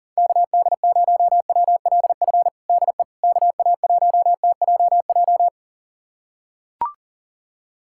Morse code, heard as a keyed tone, about 37 wpm, 700 Hz. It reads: KD0WLF DE KA1TJJ
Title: USA Callsign DE Callsign - 6 Characters - 40wpm